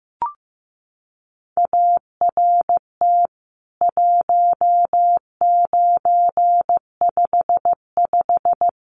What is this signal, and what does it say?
Morse code, heard as a keyed tone, about 15 wpm, 700 Hz. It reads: ART 1955
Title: CWT Exchange 15wpm